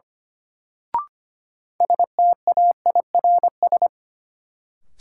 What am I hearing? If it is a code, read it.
Morse code, heard as a keyed tone, about 25 wpm, 700 Hz. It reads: STAIRS
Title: Sets of 5 Words Encore - New Word Drill - 25wpm